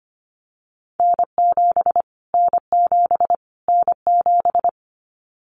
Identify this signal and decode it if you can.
Morse code, heard as a keyed tone, about 25 wpm, 700 Hz. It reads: N7 N7 N7
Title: Callsign Prefixes for Top 25 DXCC - Repeated 3x - 25wpm